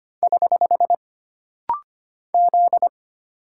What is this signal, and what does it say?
Morse code, heard as a keyed tone, about 25 wpm, 700 Hz. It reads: <HH> Z
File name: Single Character (Letter:Number:Punctuation:Prosign) - Rapid-Fire - 25wpm